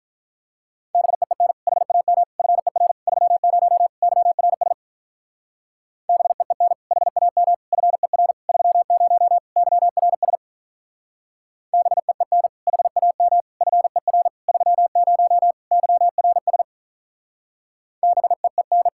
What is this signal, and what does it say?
Morse code, heard as a keyed tone, about 43 wpm, 700 Hz. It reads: BEEN HAM FER 30 YRS BEEN HAM FER 30 YRS BEEN HAM FER 30 YRS BEEN
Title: Speed-Racing Common QSO Phrases - 35wpm